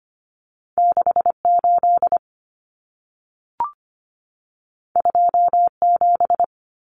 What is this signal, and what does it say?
Morse code, heard as a keyed tone, about 25 wpm, 700 Hz. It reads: 68 27
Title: Numbers - 2-Digit 25wpm